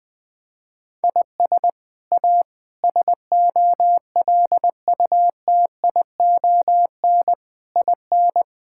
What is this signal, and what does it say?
Morse code, heard as a keyed tone, about 20 wpm, 700 Hz. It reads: IS A SOLUTION IN
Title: Sets of 8 Words - 20wpm